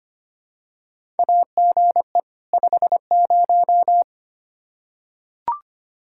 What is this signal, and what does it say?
Morse code, heard as a keyed tone, about 25 wpm, 700 Hz. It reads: AGE 50
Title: Practice QSO - Exchange 3 - More Info - 25wpm